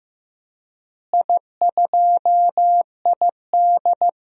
Morse code, heard as a keyed tone, about 16 wpm, 700 Hz. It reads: I2ID
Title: DXCC-100 Call Signs - 4 Characters - Repeated 2x - 15wpm